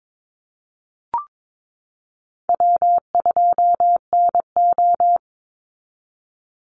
Morse code, heard as a keyed tone, about 22 wpm, 700 Hz. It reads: W2NO
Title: Easy Call Signs Repeated 2x - 22wpm